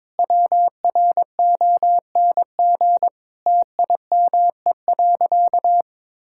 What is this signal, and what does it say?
Morse code, heard as a keyed tone, about 23 wpm, 700 Hz. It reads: WRONG TIME.
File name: practice / 22wpm / Sentences with 8-Word Phrases - 22wpm